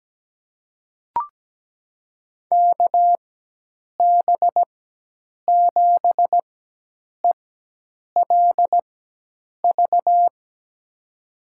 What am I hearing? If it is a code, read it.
Morse code, heard as a keyed tone, about 17 wpm, spaced as for 8 wpm, 700 Hz. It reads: KB7ELV
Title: USA Call Signs - 6 Characters - 4x Character Spacing - 17wpm